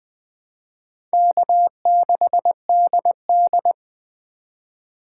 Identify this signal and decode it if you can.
Morse code, heard as a keyed tone, about 20 wpm, 700 Hz. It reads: K6DD
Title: USA Call Signs - 4 Characters - 4x Character Spacing - 20wpm